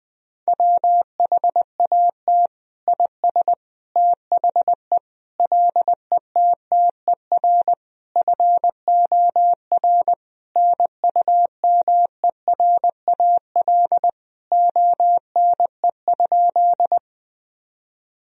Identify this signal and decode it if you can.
Morse code, heard as a keyed tone, about 20 wpm, 700 Hz. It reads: WHAT IS THE LETTER FOR NUMERAL ONE?
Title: Sentences from Top 1000 Words - Review Part 2 - 20wpm